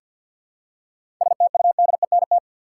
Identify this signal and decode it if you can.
Morse code, heard as a keyed tone, about 50 wpm, 700 Hz. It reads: STUDENT
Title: Seven Letter Words - 50wpm